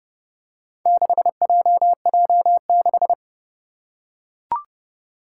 Morse code, heard as a keyed tone, about 30 wpm, 700 Hz. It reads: 6JJ6
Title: Four Challenging Letter-Number Combinations 30wpm